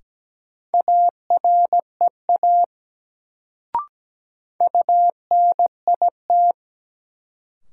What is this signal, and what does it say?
Morse code, heard as a keyed tone, about 17 wpm, 700 Hz. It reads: AREA UNIT
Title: Sentences from Top 1000 Words - Review Part 2 - New Word Drill - 17wpm